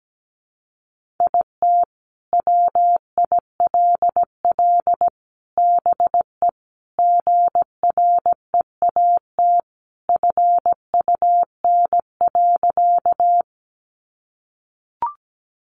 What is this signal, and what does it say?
Morse code, heard as a keyed tone, about 17 wpm, 700 Hz. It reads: IT WILL BE GREAT FUN.
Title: Sentences from Top 700 Words 17wpm